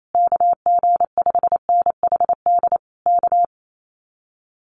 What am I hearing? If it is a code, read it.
Morse code, heard as a keyed tone, about 28 wpm, 700 Hz. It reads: KG5NHD K
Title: Common QSO Phrases Encore 28wpm